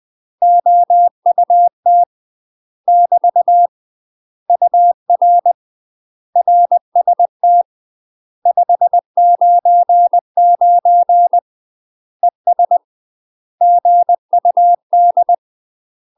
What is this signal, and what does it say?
Morse code, heard as a keyed tone, about 20 wpm, 700 Hz. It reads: OUT <BT> UR RST 599 ES GUD
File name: UK MASTER QSOs TRAINING / 20WPM / UK MASTER QSOs 0028